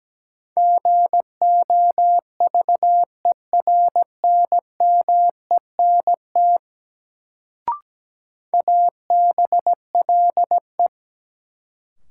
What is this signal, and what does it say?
Morse code, heard as a keyed tone, about 17 wpm, 700 Hz. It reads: GOVERNMENT ABLE
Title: Sets of 2 Words - New Word Drill - 17wpm